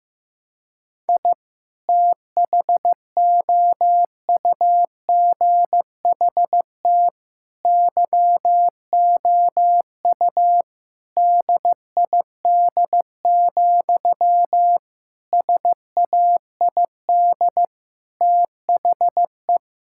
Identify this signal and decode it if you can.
Morse code, heard as a keyed tone, about 16 wpm, 700 Hz. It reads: I THOUGHT YOU DID, SAID THE
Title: Sentences from Top 1000 Words - Review Part 5 - 15wpm